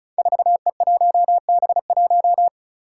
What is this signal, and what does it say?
Morse code, heard as a keyed tone, about 36 wpm, 700 Hz. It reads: 4E1B1
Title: Five Challenging Letter-Number Combinations 35wpm